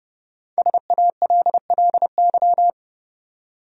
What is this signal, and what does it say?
Morse code, heard as a keyed tone, about 30 wpm, 700 Hz. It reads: SALLY
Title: USA Common Names - 30wpm